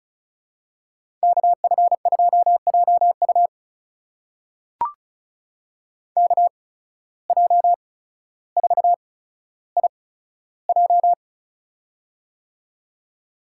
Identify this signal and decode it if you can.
Morse code, heard as a keyed tone, about 36 wpm, spaced as for 15 wpm, 700 Hz. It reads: KF2JU KJ4IJ
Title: USA Call Signs - 5 Characters - 8x Character Spacing - 35wpm